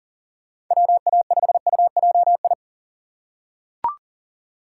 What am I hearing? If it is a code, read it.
Morse code, heard as a keyed tone, about 40 wpm, 700 Hz. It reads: WA5UJI
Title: USA Call Signs - 6 Characters - 4x Character Spacing - 40wpm